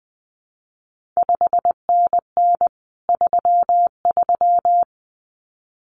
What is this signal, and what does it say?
Morse code, heard as a keyed tone, about 19 wpm, 700 Hz. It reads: 5NN 33
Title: CQ WW DX Contest Exchange Encore - 20wpm